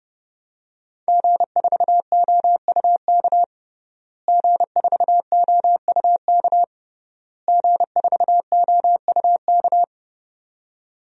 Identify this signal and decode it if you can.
Morse code, heard as a keyed tone, about 32 wpm, 700 Hz. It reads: G4OUK G4OUK G4OUK
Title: DXCC-100 Call Signs - 5 Characters - Repeated 3x - 30wpm